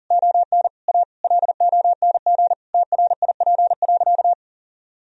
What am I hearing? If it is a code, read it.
Morse code, heard as a keyed tone, about 40 wpm, 700 Hz. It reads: ON A LONG TRIP.
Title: Sentences from Top 1000 Words - Review Part 4 - 40wpm